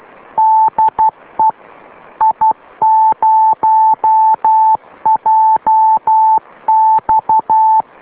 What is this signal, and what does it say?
Morse code, heard as a keyed tone, about 12 wpm, 880 Hz. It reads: DE I0JX